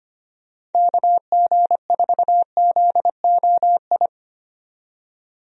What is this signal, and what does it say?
Morse code, heard as a keyed tone, about 25 wpm, 700 Hz. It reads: KG4ZOI
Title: USA Call Signs - 6 Characters - 25wpm